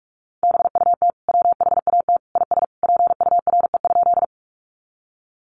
Morse code, heard as a keyed tone, about 44 wpm, 700 Hz. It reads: BUT WHAT IS PURE?
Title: Sentences from Top 1500 Words 45wpm